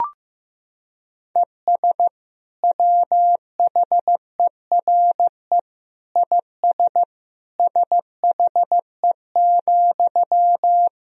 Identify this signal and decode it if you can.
Morse code, heard as a keyed tone, about 15 wpm, 700 Hz. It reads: ES WHERE IS SHE,
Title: Sentences from Top 200 Words and Abbreviations 15wpm